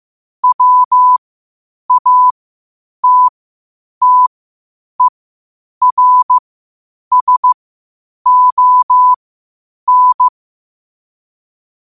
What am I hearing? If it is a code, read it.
Morse code, heard as a keyed tone, about 15 wpm, spaced as for 8 wpm, 1000 Hz. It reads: WATTERSON
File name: qotdfiles / 8wpm